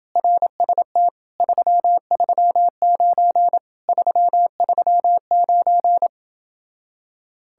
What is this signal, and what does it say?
Morse code, heard as a keyed tone, about 30 wpm, 700 Hz. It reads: RST 339 339